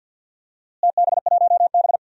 Morse code, heard as a keyed tone, about 51 wpm, 700 Hz. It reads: TB1B
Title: DXCC-100 Call Signs - 4 Characters - Repeated 2x - 50wpm